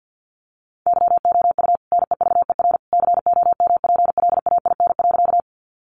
Morse code, heard as a keyed tone, about 50 wpm, 700 Hz. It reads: YOU NEVER COMPLAIN.
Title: Sentences from Top 1300 Words 50wpm